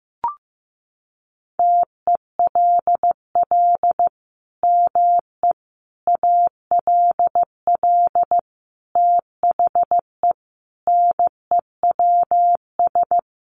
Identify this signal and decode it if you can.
Morse code, heard as a keyed tone, about 15 wpm, 700 Hz. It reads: TELL ME ALL THE NEWS
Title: Sentences from Top 800 Words 15wpm